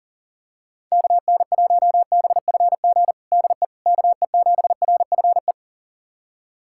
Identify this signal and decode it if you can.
Morse code, heard as a keyed tone, about 40 wpm, 700 Hz. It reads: KN1BFG DE KE7RFE
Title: USA Callsign DE Callsign - 6 Characters - 40wpm